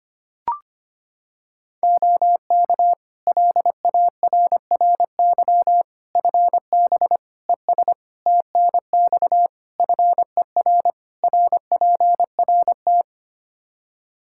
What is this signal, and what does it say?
Morse code, heard as a keyed tone, about 25 wpm, 700 Hz. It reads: OK LARRY FB ES TNX FER RPRT
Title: Practice QSO - Exchange 2 - Info - 25wpm